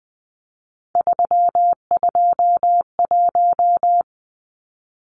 Morse code, heard as a keyed tone, about 20 wpm, 700 Hz. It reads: 321
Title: Numbers - 3-Digit 20wpm